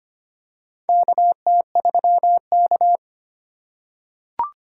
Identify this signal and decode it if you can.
Morse code, heard as a keyed tone, about 25 wpm, 700 Hz. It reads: KT3K
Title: USA Call Signs - 4 Characters - 2x Character Spacing - 25wpm